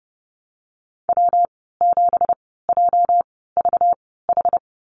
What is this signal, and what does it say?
Morse code, heard as a keyed tone, about 30 wpm, spaced as for 17 wpm, 700 Hz. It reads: W7JVH
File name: Lesson 37g - Letter x - Call Signs 30wpm